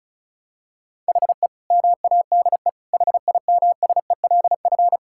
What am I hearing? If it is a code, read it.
Morse code, heard as a keyed tone, about 36 wpm, 700 Hz. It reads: HE MADE HIMSELF
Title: Sentences from Top 1600 Words 35wpm